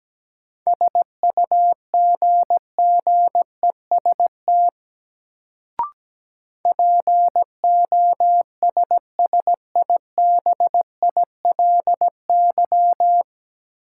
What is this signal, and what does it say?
Morse code, heard as a keyed tone, about 17 wpm, 700 Hz. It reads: SUGGEST POSSIBILY
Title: Sentences from Top 1300 Words - New Word Drill Alt - 17wpm